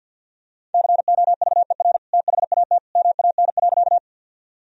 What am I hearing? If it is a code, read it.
Morse code, heard as a keyed tone, about 50 wpm, 700 Hz. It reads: COVER THAT MAN.
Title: Sentences from Top 1000 Words - Review Part 1 - 50wpm